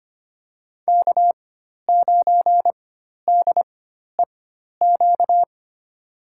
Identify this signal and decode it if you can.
Morse code, heard as a keyed tone, about 25 wpm, spaced as for 12 wpm, 700 Hz. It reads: K9DEQ
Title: USA Call Signs - 5 Characters - 4x Character Spacing - 25wpm